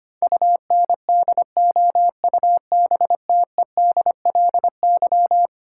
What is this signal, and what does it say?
Morse code, heard as a keyed tone, about 25 wpm, 700 Hz. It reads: UNDOUBTEDLY
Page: morse-code-ninja-practice-25wpm